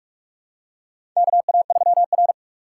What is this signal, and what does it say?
Morse code, heard as a keyed tone, about 44 wpm, 700 Hz. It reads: KA3R
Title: USA Call Signs - 6x Character Spacing - 45wpm